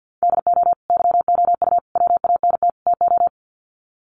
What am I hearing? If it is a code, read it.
Morse code, heard as a keyed tone, about 48 wpm, 700 Hz. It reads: DO YOU WANT TO
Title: Sets of 4 Words 50wpm